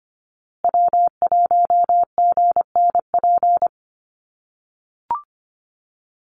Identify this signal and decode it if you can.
Morse code, heard as a keyed tone, about 25 wpm, 700 Hz. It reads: W1GNP
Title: USA Call Signs Repeated 3x 25wpm